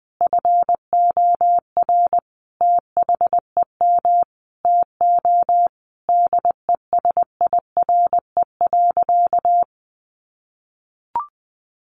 Morse code, heard as a keyed tone, about 20 wpm, 700 Hz. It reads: FOR THEM TO DESIRE.
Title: Sentences from Top 1100 Words 20wpm